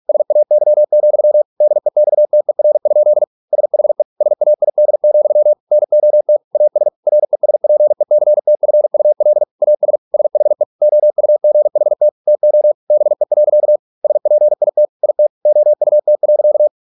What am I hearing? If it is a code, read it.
Morse code, heard as a keyed tone, about 46 wpm, 600 Hz. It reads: SAY, DEXTER? SHE SAID, NOT AS RESPECTFUL AS SHE OUGHT TO BE. SPIT IT OUT.